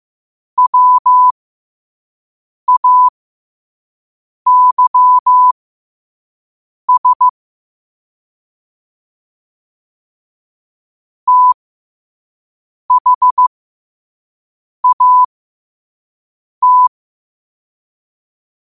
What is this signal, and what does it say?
Morse code, heard as a keyed tone, about 15 wpm, spaced as for 5 wpm, 1000 Hz. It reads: WAYS THAT
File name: qotdfiles / 5wpm